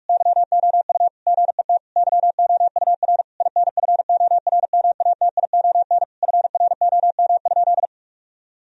Morse code, heard as a keyed tone, about 43 wpm, 700 Hz. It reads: YOU GET YOUR INFORMATION FROM?
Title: Sentences from Top 1000 Words - Review Part 3 - 45wpm